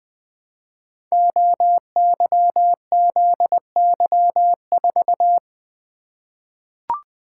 Morse code, heard as a keyed tone, about 20 wpm, 700 Hz. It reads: OYZY4
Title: Code Groups - 5 Letter:Numbers - 20wpm